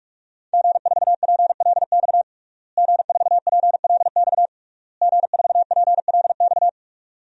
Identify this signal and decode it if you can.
Morse code, heard as a keyed tone, about 46 wpm, 700 Hz. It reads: G4PLX G4PLX G4PLX
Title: DXCC-100 Call Signs - 5 Characters - Repeated 3x - 45wpm